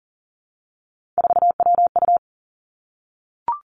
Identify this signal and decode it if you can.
Morse code, heard as a keyed tone, about 39 wpm, 690 Hz. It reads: 4WU
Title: Three Challenging Letter-Number Combinations 40wpm